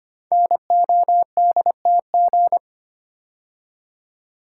Morse code, heard as a keyed tone, about 25 wpm, 700 Hz. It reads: NODTG
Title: Code Groups - 5 Letters - 25wpm